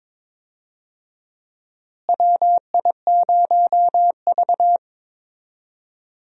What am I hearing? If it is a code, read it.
Morse code, heard as a keyed tone, about 21 wpm, 700 Hz. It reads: WI0V
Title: USA Call Signs - 4 Characters - Repeated 2x - 22wpm